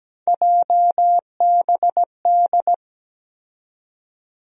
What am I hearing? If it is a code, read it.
Morse code, heard as a keyed tone, about 17 wpm, 700 Hz. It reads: JBD
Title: USA Call Sign Postfixes - 3 Letters 17wpm